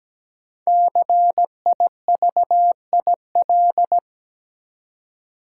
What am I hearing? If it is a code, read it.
Morse code, heard as a keyed tone, about 17 wpm, 700 Hz. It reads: CIVIL
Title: Five Letter Words - 17wpm